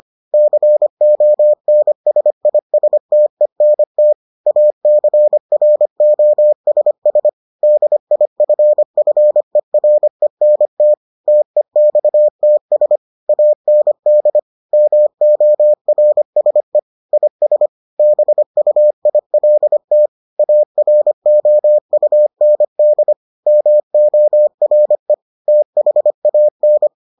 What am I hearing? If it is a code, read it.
Morse code, heard as a keyed tone, about 25 wpm, 600 Hz. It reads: CONSISTENT ACROSS DIFFERENT TEXTS AND MORSE IS BUILT AROUND MORE THAN